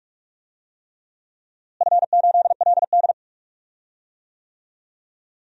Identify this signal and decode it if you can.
Morse code, heard as a keyed tone, about 41 wpm, 700 Hz. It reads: F8LD
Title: DXCC-100 Call Signs - 4 Characters - Repeated 3x - 45wpm